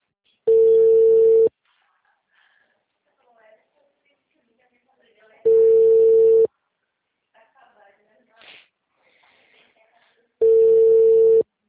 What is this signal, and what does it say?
Ringback tone (China, 450 Hz), with speech.